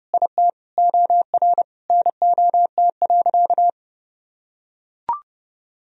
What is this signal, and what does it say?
Morse code, heard as a keyed tone, about 32 wpm, 700 Hz. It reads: IT OR NOT.
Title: Sentences from Top 1000 Words - Review Part 3 - 30wpm